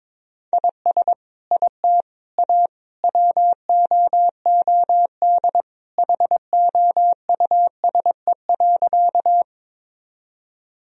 Morse code, heard as a keyed tone, about 21 wpm, 700 Hz. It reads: IS IT A WOOD HOUSE.
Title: Sentences from Top 1000 Words - Review Part 2 - 22wpm